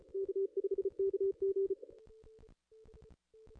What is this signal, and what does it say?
Morse code, heard as a keyed tone, about 37 wpm, 380 Hz. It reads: K5KG
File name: contests / ss-cw-k5tr-op-n5rz -0200